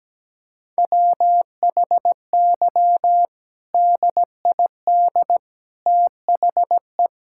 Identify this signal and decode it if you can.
Morse code, heard as a keyed tone, about 17 wpm, 700 Hz. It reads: WHY DID THE
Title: Sentences from Top 1000 Words - Review Part 5 - 17wpm